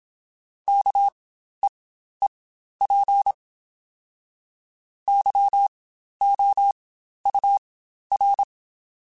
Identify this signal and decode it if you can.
Morse code, heard as a keyed tone, about 27 wpm, spaced as for 12 wpm, 800 Hz. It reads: KEEP YOUR
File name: Spaced / Chuckles Spaced-mp3s / SoftWords sp @25